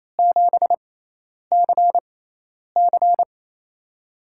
Morse code, heard as a keyed tone, about 28 wpm, spaced as for 10 wpm, 700 Hz. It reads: 7CC
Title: USA Call Signs - 6x Character Spacing - 28wpm